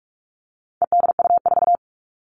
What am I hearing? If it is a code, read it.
Morse code, heard as a keyed tone, about 44 wpm, 700 Hz. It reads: EDU4
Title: All Cut Numbers - 4-Digit 45wpm